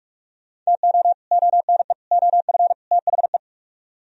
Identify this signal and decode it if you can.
Morse code, heard as a keyed tone, about 58 wpm, 700 Hz. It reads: TO ONE OF THE